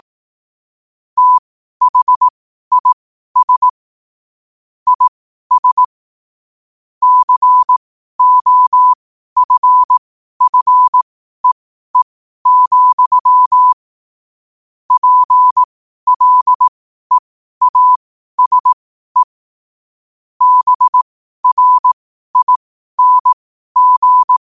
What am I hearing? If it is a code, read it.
Morse code, heard as a keyed tone, about 18 wpm, spaced as for 12 wpm, 1000 Hz. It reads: THIS IS COFFEE, PLEASE BRING
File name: qotdfiles / 12wpm